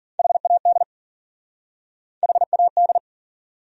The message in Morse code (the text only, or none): HAD HAD